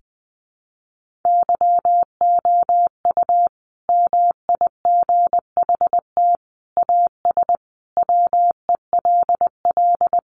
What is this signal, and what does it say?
Morse code, heard as a keyed tone, about 20 wpm, 700 Hz. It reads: YOU MIGHT AS WELL
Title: Sentences from Top 1400 Words 20wpm